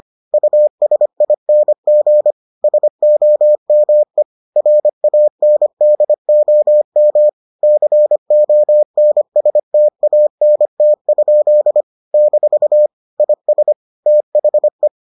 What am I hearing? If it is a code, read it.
Morse code, heard as a keyed tone, about 25 wpm, 600 Hz. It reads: USING SOME RANDOM CONSTANT? - IS THE